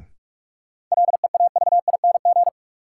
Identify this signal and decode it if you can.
Morse code, heard as a keyed tone, about 42 wpm, 700 Hz. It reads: LEAVING
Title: Sentences from Top 1000 Words - Review Part 4 - New Word Drill - 45wpm